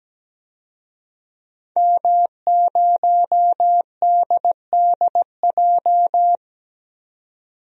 Morse code, heard as a keyed tone, about 17 wpm, 700 Hz. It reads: M0DDJ
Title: DXCC-100 Call Signs - 5 Characters - 17wpm